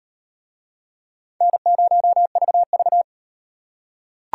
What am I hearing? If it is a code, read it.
Morse code, heard as a keyed tone, about 38 wpm, 700 Hz. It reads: N0VV